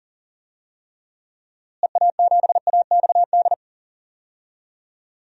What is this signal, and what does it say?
Morse code, heard as a keyed tone, about 40 wpm, 700 Hz. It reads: EA7AXD
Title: DXCC-100 Call Signs - 6 Characters - Repeated 3x - 40wpm